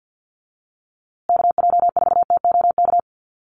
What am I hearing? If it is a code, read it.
Morse code, heard as a keyed tone, about 50 wpm, 700 Hz. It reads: KJ4TOK
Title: USA Call Signs Repeated 3x 50wpm